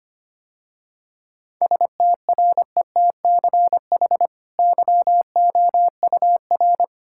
Morse code, heard as a keyed tone, about 25 wpm, 690 Hz. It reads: STRETCH YOUR
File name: Easy Sentences with Hints - 25wpm